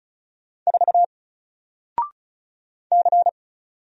Morse code, heard as a keyed tone, about 35 wpm, 700 Hz. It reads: 4 C
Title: Single Character (Letter:Number:Punctuation:Prosign) Alt 35wpm